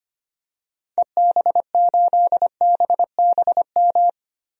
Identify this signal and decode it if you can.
Morse code, heard as a keyed tone, about 25 wpm, 700 Hz. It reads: EB8BBM
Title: DXCC-100 Call Signs - 6 Characters - Repeated 2x - 25wpm